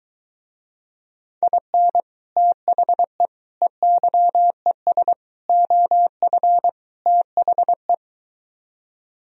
Morse code, heard as a keyed tone, about 23 wpm, 700 Hz. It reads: IN THE EYES OF THE